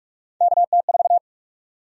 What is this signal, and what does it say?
Morse code, heard as a keyed tone, about 45 wpm, 700 Hz. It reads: KT4
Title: USA Call Sign Prefixes - Repeated 2x - 45wpm